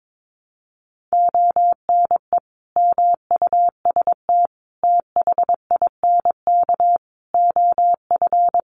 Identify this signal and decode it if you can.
Morse code, heard as a keyed tone, about 22 wpm, 700 Hz. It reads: ONE MUST THINK OF
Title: Sentences from Top 700 Words 22wpm